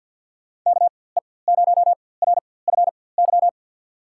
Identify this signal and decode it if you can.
Morse code, heard as a keyed tone, about 53 wpm, spaced as for 24 wpm, 700 Hz. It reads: KE0RFY
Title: USA Call Signs - 6 Characters - 4x Character Spacing - 50wpm